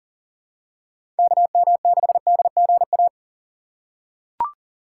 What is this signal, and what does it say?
Morse code, heard as a keyed tone, about 40 wpm, 700 Hz. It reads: KM6DGA
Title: USA Call Signs - 6 Characters - 2x Character Spacing - 40wpm